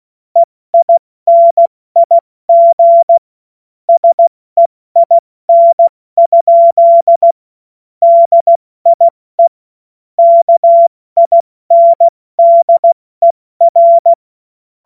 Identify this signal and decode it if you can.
Morse code, heard as a keyed tone, about 16 wpm, 680 Hz. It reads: EINIG SEIN? DIE KINDER
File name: Morsekurs / Witze / Witz 21. März 2023 70 F80